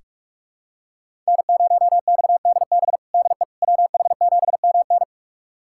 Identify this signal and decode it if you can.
Morse code, heard as a keyed tone, about 47 wpm, 700 Hz. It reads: N0XDB DE WH7MN
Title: USA Callsign DE Callsign - 5 Characters - 45wpm